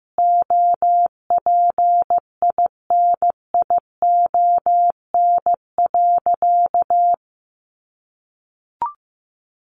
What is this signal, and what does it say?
Morse code, heard as a keyed tone, about 15 wpm, 700 Hz. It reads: OPINION.
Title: Sentences from Top 1200 Words 15wpm